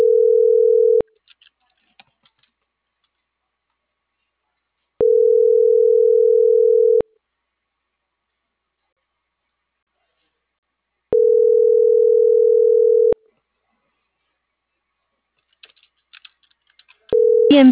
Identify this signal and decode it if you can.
Ringback tone (North America), 440 + 480 Hz.